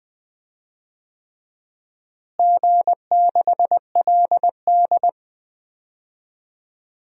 Morse code, heard as a keyed tone, about 21 wpm, 700 Hz. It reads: G6LD
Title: DXCC-100 Call Signs - 4 Characters - Repeated 3x - 20wpm